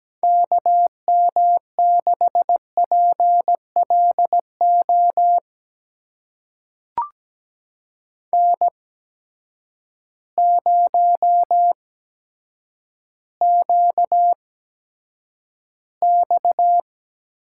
Morse code, heard as a keyed tone, about 17 wpm, spaced as for 8 wpm, 700 Hz. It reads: KM6PLO N0QX
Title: USA Call Signs - 8x Character Spacing - 17wpm